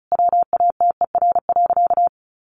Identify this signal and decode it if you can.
Morse code, heard as a keyed tone, about 34 wpm, 700 Hz. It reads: WATER.
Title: Sentences from Top 1300 Words 35wpm